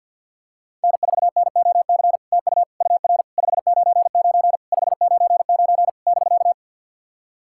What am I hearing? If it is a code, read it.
Morse code, heard as a keyed tone, about 50 wpm, 700 Hz. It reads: N4NOC TU UR 599 599 <BK>
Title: SOTA QSO Practice 50wpm